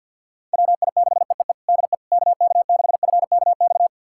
Morse code, heard as a keyed tone, about 51 wpm, 700 Hz. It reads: WI6EEE DE KK6FKX
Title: USA Callsign DE Callsign - 6 Characters - 50wpm